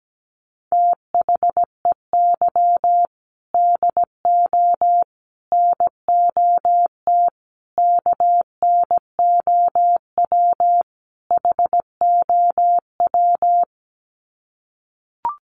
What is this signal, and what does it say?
Morse code, heard as a keyed tone, about 17 wpm, 700 Hz. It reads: THEY DO NOT KNOW HOW
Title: Sets of 5 Words Encore2 17wpm